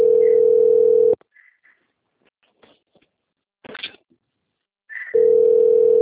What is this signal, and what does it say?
Ringback tone (North America), 440 + 480 Hz.
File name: out- - - - - .13668